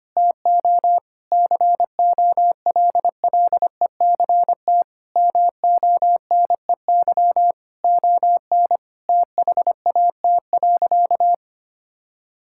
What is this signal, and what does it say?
Morse code, heard as a keyed tone, about 25 wpm, 700 Hz. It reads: TO COLLECT MONEY ON THAT.
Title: Sentences from Top 1600 Words 25wpm